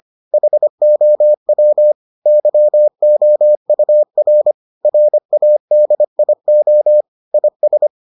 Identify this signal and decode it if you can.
Morse code, heard as a keyed tone, about 25 wpm, 600 Hz. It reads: HOW YOUR RADIO IS